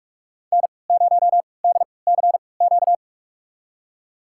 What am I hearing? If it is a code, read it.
Morse code, heard as a keyed tone, about 45 wpm, spaced as for 25 wpm, 700 Hz. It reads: N0DCQ
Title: USA Call Signs - 3x Character Spacing - 45wpm